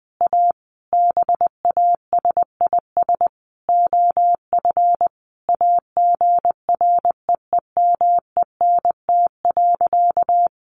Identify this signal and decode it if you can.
Morse code, heard as a keyed tone, about 20 wpm, 700 Hz. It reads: A BASIS OF AGREEMENT.
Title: Sentences from Top 1500 Words 20wpm